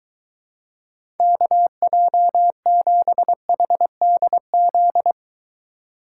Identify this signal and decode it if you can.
Morse code, heard as a keyed tone, about 23 wpm, 700 Hz. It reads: KJ7HDZ